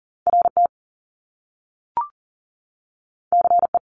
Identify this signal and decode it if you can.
Morse code, heard as a keyed tone, about 42 wpm, 700 Hz. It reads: RT CE
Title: Top Two Letter Combinations in Words 40wpm